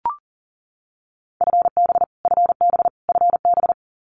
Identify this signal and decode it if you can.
Morse code, heard as a keyed tone, about 40 wpm, 700 Hz. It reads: FB FB FB
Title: Abbreviations Repeated 3x - 40wpm